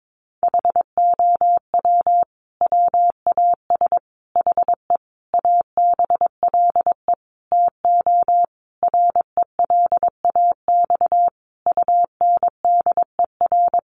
Morse code, heard as a keyed tone, about 22 wpm, 700 Hz. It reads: HOW WAS HE ABLE TO RELAX UNDER
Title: Sentences from Top 1400 Words 22wpm